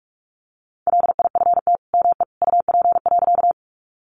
Morse code, heard as a keyed tone, about 44 wpm, 700 Hz. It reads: LIFT ME UP.
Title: Sentences from Top 1000 Words 45wpm